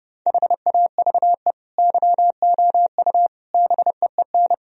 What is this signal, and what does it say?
Morse code, heard as a keyed tone, about 30 wpm, 700 Hz. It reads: HAVE YOU BEEN
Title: Sentences from Top 1000 Words - Review Part 4 - 30wpm